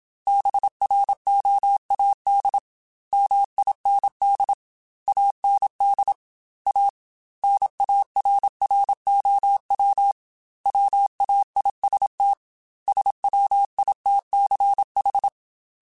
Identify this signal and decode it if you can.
Morse code, heard as a keyed tone, about 27 wpm, 800 Hz. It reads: BROAD MIND AND A NARROW WAIST SWITCH
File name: Chuckles / Chuckles-mp3s / MiddleAge @25